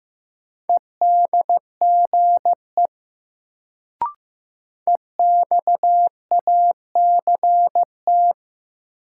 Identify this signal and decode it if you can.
Morse code, heard as a keyed tone, about 15 wpm, 700 Hz. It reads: EDGE EXACT
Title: Sentences from Top 1300 Words - New Word Drill - 15wpm